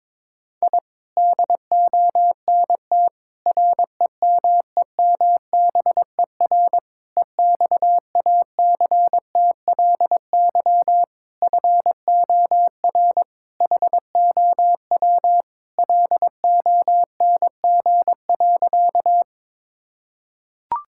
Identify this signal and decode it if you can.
Morse code, heard as a keyed tone, about 22 wpm, 700 Hz. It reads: I DONT REMEMBER EXACTLY FOR HOW LONG.
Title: Sentences from Top 1000 Words - Review Part 6 - 22wpm